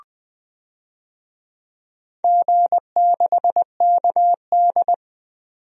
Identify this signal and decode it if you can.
Morse code, heard as a keyed tone, about 21 wpm, 700 Hz. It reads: G6KD
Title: DXCC-100 Call Signs - 4 Characters - 20wpm